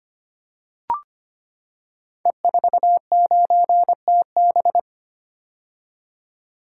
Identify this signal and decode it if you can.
Morse code, heard as a keyed tone, about 25 wpm, 700 Hz. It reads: E49TB
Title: Code Groups - 5 Letter:Numbers - 25wpm